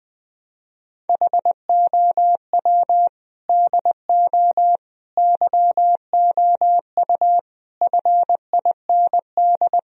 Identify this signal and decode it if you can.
Morse code, heard as a keyed tone, about 20 wpm, 700 Hz. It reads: HOW DO YOU FIND
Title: Sentences from Top 1000 Words - Review Part 4 - 20wpm